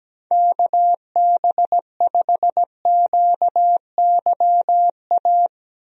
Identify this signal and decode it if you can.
Morse code, heard as a keyed tone, about 17 wpm, 700 Hz. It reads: KB5QYA
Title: USA Call Signs - 6 Characters - 17wpm